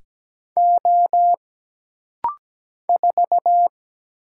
Morse code, heard as a keyed tone, about 17 wpm, 700 Hz. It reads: O 4
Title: Single Letter-Number Alt - Rapid-Fire 17wpm